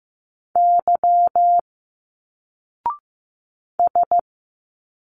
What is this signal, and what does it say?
Morse code, heard as a keyed tone, about 15 wpm, 700 Hz. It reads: Y S
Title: Single Letters 15wpm